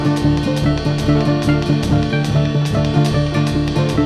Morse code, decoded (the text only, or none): none